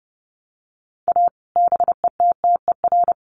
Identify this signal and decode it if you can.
Morse code, heard as a keyed tone, about 30 wpm, 700 Hz. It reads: A BETTER
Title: Sets of 5 Words Encore2 30wpm